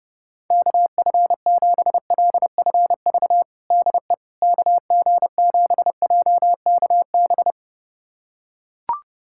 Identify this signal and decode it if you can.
Morse code, heard as a keyed tone, about 30 wpm, 700 Hz. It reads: KF7LFV DE KG7JKB
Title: USA Callsign DE Callsign 30wpm